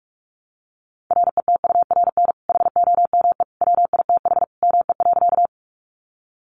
Morse code, heard as a keyed tone, about 44 wpm, 700 Hz. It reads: RETURN HOME WITH ME.
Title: Sentences from Top 800 Words 45wpm